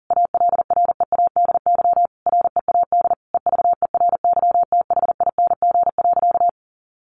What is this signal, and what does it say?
Morse code, heard as a keyed tone, about 40 wpm, 700 Hz. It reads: ALREADY READ EVERYTHING.
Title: Sentences from Top 700 Words 40wpm